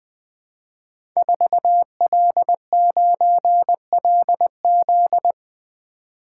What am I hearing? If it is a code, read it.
Morse code, heard as a keyed tone, about 20 wpm, 700 Hz. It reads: 4L9LZ
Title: Code Groups - 5 Letter:Numbers - 20wpm